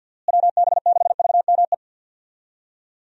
Morse code, heard as a keyed tone, about 46 wpm, 700 Hz. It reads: WB6VME
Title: USA Call Signs - 6 Characters - Repeated 2x - 50wpm